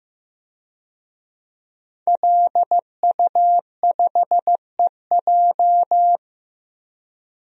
Morse code, heard as a keyed tone, about 15 wpm, 700 Hz. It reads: LU5EJ
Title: DXCC-100 Call Signs - 5 Characters - Repeated 2x - 15wpm